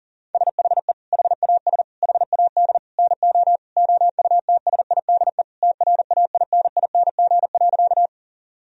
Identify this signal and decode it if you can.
Morse code, heard as a keyed tone, about 40 wpm, 700 Hz. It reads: SHE HAS HAD NO OUTSIDE TRAINING.
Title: Sentences from Top 1000 Words - Review Part 4 - 40wpm